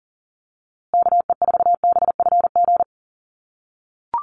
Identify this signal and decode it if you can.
Morse code, heard as a keyed tone, about 40 wpm, 700 Hz. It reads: KE4BFG
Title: USA Call Signs Repeated 3x 40wpm